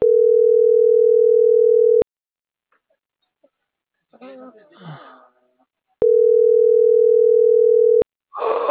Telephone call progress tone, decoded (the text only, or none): ringback tone